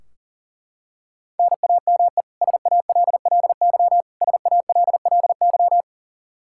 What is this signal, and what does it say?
Morse code, heard as a keyed tone, about 37 wpm, 700 Hz. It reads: NAME SALLY SALLY
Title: K1USN SST QSO Practice 40wpm